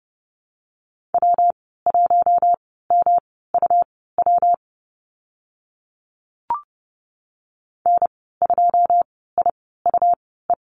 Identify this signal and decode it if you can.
Morse code, heard as a keyed tone, about 29 wpm, spaced as for 17 wpm, 700 Hz. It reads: W1MUW N2IUE
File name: Lesson 19g - Letter w - Call Signs 30wpm